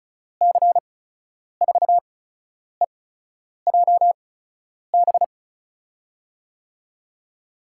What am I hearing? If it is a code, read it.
Morse code, heard as a keyed tone, about 36 wpm, spaced as for 10 wpm, 700 Hz. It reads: C4EJB
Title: USA Call Signs - 8x Character Spacing - 35wpm